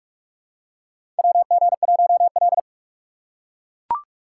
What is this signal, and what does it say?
Morse code, heard as a keyed tone, about 43 wpm, 700 Hz. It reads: WG1L
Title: USA Call Signs - 4 Characters - 4x Character Spacing - 45wpm